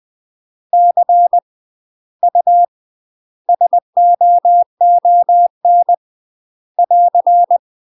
Morse code, heard as a keyed tone, about 20 wpm, 700 Hz. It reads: C U SOON <AR>
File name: UK MASTER QSOs TRAINING / 20WPM / UK MASTER QSOs 0025